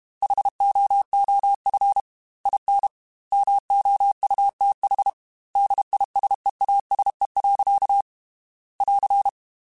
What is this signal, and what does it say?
Morse code, heard as a keyed tone, about 32 wpm, 800 Hz. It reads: HOOF IN MOUTH DISEASE. <AR>
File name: Quotes / Quotes-mp3s / HoofInMouth @30